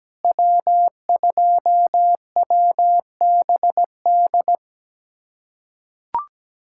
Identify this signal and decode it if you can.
Morse code, heard as a keyed tone, about 17 wpm, 700 Hz. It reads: W2WBD
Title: USA Callsign DE Callsign - 5 Characters - 17wpm